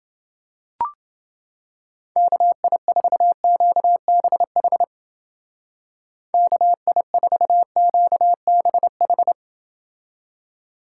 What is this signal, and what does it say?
Morse code, heard as a keyed tone, about 28 wpm, 700 Hz. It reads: KI4QBH KI4QBH